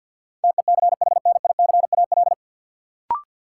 Morse code, heard as a keyed tone, about 50 wpm, 700 Hz. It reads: TECHNICAL
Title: Sentences from Top 1100 Words - New Word Drill - 50wpm